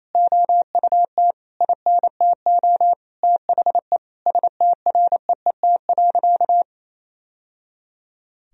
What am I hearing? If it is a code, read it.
Morse code, heard as a keyed tone, about 28 wpm, 700 Hz. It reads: OUT INTO THE STREET.
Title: Sentences from Top 1000 Words - Review Part 3 - 28wpm